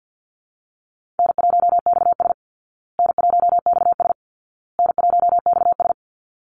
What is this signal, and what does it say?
Morse code, heard as a keyed tone, about 50 wpm, 700 Hz. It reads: N1XS N1XS N1XS
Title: Easy Call Signs Repeated 3x - 50wpm